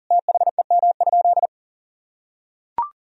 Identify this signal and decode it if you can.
Morse code, heard as a keyed tone, about 40 wpm, 700 Hz. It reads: THEM?
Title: Sentences from Top 1000 Words - Review Part 3 - 40wpm